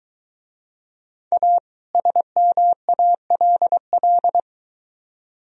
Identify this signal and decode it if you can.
Morse code, heard as a keyed tone, about 22 wpm, 700 Hz. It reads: A SMALL